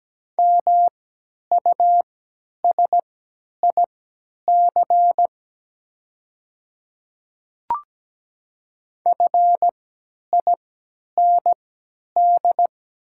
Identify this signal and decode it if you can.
Morse code, heard as a keyed tone, about 17 wpm, spaced as for 10 wpm, 700 Hz. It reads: MUSIC FIND
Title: 3x Character Spacing - Top 300 Words - 17wpm